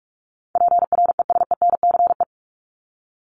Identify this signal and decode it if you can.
Morse code, heard as a keyed tone, about 44 wpm, 700 Hz. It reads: PRESENCE
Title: Top 1751-2000 Words 45wpm